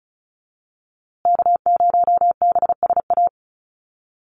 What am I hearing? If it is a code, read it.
Morse code, heard as a keyed tone, about 34 wpm, 700 Hz. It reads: K0BSA
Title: USA Call Signs Repeated 3x 35wpm